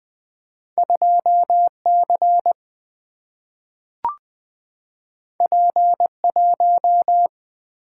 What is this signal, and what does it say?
Morse code, heard as a keyed tone, about 20 wpm, 700 Hz. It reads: 2C P1
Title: Single-Letter Single-Number 20wpm